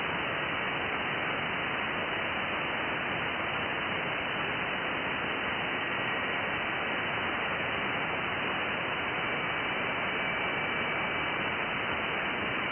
FT8.
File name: IC-7100